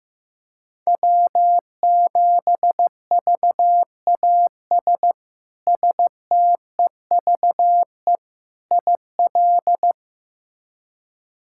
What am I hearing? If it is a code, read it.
Morse code, heard as a keyed tone, about 16 wpm, 700 Hz. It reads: W7VAS STEVE IL
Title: K1USN SST QSO Practice - 15wpm